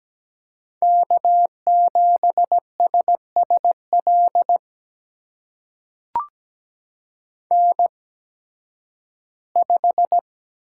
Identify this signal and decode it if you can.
Morse code, heard as a keyed tone, about 17 wpm, spaced as for 11 wpm, 700 Hz. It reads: K7SSL N5
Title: USA Call Signs - 5 Characters - 8x Character Spacing - 17wpm